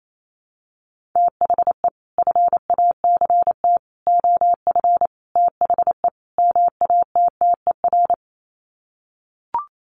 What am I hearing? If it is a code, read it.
Morse code, heard as a keyed tone, about 28 wpm, 700 Hz. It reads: THE FACT OF THE MATTER
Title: Sets of 5 Words 28wpm